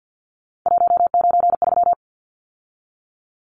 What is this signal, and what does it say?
Morse code, heard as a keyed tone, about 50 wpm, 700 Hz. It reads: 193
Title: Numbers - 3-Digit 50wpm